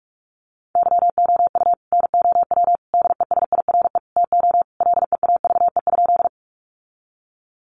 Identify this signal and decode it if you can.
Morse code, heard as a keyed tone, about 46 wpm, 700 Hz. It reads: YOU NOW DESIRE TO LEAVE?
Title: Sentences from Top 1100 Words 45wpm